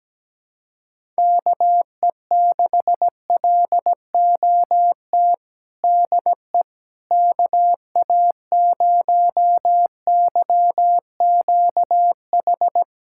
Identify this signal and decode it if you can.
Morse code, heard as a keyed tone, about 17 wpm, 700 Hz. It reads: KE6LOT DE KA0YQH
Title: USA Callsign DE Callsign 17wpm